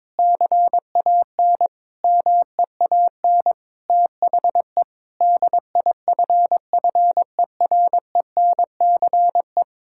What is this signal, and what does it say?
Morse code, heard as a keyed tone, about 22 wpm, 700 Hz. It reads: CAN MEAN THE DIFFERENCE
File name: practice / 22wpm / Sentences with 8-Word Phrases - 22wpm